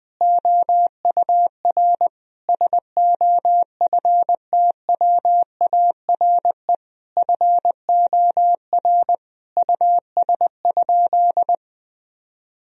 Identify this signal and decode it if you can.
Morse code, heard as a keyed tone, about 20 wpm, 700 Hz. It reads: OUR SOFTWARE FOR US?
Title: Sentences from Top 1000 Words - Review Part 3 - 20wpm